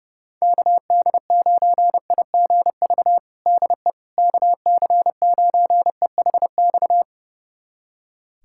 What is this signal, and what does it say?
Morse code, heard as a keyed tone, about 32 wpm, 700 Hz. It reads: KD9IGV DE KC9EHX
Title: USA Callsign DE Callsign - 6 Characters - 30wpm